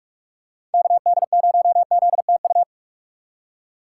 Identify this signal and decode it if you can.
Morse code, heard as a keyed tone, about 46 wpm, 700 Hz. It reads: KD0ZTU
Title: USA Call Signs - 3x Character Spacing - 45wpm